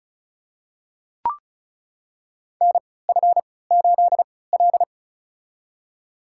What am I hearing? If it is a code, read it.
Morse code, heard as a keyed tone, about 36 wpm, spaced as for 20 wpm, 700 Hz. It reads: NF8L
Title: USA Call Signs - 3x Character Spacing - 35wpm